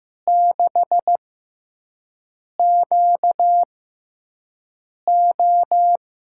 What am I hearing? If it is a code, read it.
Morse code, heard as a keyed tone, about 15 wpm, spaced as for 5 wpm, 700 Hz. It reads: 6QO
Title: USA Call Signs - 6x Character Spacing - 15wpm